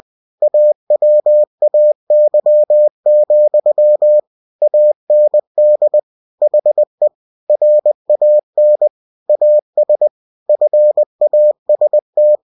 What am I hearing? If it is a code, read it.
Morse code, heard as a keyed tone, about 20 wpm, 600 Hz. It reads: AWAY, AND HE RAN AS FAST